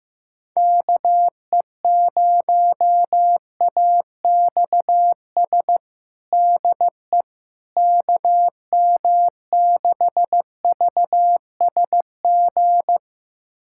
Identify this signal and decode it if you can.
Morse code, heard as a keyed tone, about 15 wpm, 700 Hz. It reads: KE0AXS DE KM6VSG
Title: USA Callsign DE Callsign 15wpm